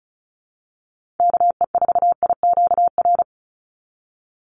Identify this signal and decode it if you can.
Morse code, heard as a keyed tone, about 34 wpm, 700 Hz. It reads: KE4IQR
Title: USA Call Signs 35wpm